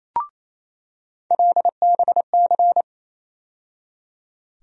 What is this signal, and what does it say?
Morse code, heard as a keyed tone, about 28 wpm, 700 Hz. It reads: LBC
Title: USA Call Sign Postfixes - 3 Letters 28wpm